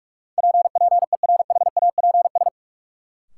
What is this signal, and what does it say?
Morse code, heard as a keyed tone, about 47 wpm, 700 Hz. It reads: PPERHAPS
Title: Sentences from Top 1100 Words - New Word Drill - 45wpm